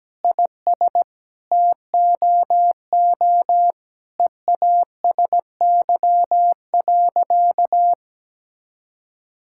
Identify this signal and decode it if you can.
Morse code, heard as a keyed tone, about 17 wpm, 700 Hz. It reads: IS TOO EASY.
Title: Sentences from Top 1000 Words - Review Part 5 - 17wpm